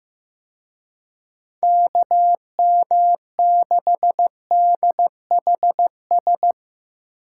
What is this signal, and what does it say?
Morse code, heard as a keyed tone, about 17 wpm, 700 Hz. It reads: KM6DHS
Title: Speed Racing - USA 2x3 Call Signs 15wpm